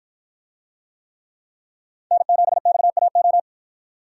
Morse code, heard as a keyed tone, about 53 wpm, 700 Hz. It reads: N7XAO